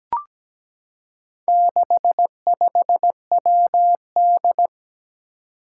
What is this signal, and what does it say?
Morse code, heard as a keyed tone, about 17 wpm, 700 Hz. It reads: 65WD
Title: Four Challenging Letter-Number Combinations 17wpm